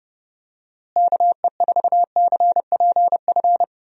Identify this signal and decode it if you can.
Morse code, heard as a keyed tone, about 31 wpm, 700 Hz. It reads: KE4CPF
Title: USA Call Signs - 6 Characters - Repeated 2x - 30wpm